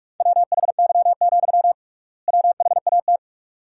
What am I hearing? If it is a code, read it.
Morse code, heard as a keyed tone, about 42 wpm, 700 Hz. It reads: WHY, WHAT
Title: 3x Word-Spacing - Sentences from Top 200 Words - 45wpm